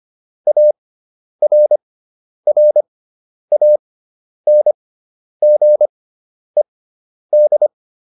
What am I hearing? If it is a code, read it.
Morse code, heard as a keyed tone, about 25 wpm, spaced as for 10 wpm, 600 Hz. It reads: ARRANGED